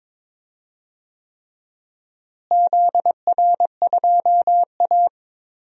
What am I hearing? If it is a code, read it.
Morse code, heard as a keyed tone, about 21 wpm, 700 Hz. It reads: ZR2A
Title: DXCC-100 Call Signs - 4 Characters - Repeated 3x - 22wpm